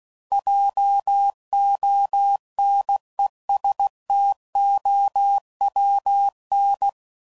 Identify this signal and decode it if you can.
Morse code, heard as a keyed tone, about 16 wpm, 800 Hz. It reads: JONESTOWN
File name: CW Code Groups @15WPM / fromSM's / Cities